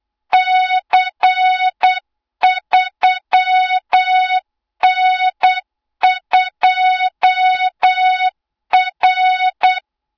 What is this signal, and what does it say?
Morse code, heard as a keyed tone, about 8 wpm, 3700 Hz, with noise. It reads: C3N2R
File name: Morze-HA2MN / hang / 20 GYAKORLAT